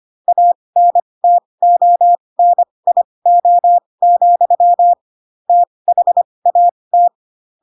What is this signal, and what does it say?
Morse code, heard as a keyed tone, about 25 wpm, spaced as for 20 wpm, 700 Hz. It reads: ANTONIO, THAT
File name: Pinocchio / Pinocchio ch2 15 25 ik1hge